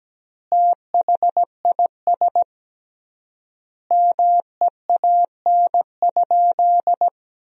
Morse code, heard as a keyed tone, about 17 wpm, 700 Hz. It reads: THIS MEAN?
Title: 3x Word-Spacing - Sentences from Top 200 Words - 17wpm